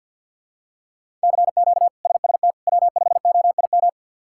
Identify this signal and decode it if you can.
Morse code, heard as a keyed tone, about 51 wpm, 690 Hz. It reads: CQ SST W5OIM
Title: K1USN SST QSO Practice - 50wpm